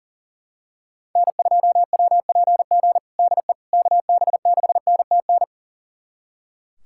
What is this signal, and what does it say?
Morse code, heard as a keyed tone, about 40 wpm, 700 Hz. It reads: N2WPG DE KB6NTN
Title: USA Callsign DE Callsign 40wpm